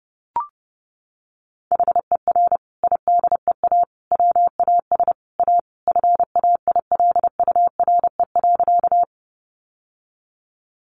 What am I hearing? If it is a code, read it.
Morse code, heard as a keyed tone, about 30 wpm, 700 Hz. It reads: HER IDEA WAS A FAILURE.
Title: Sentences from Top 900 Words 30wpm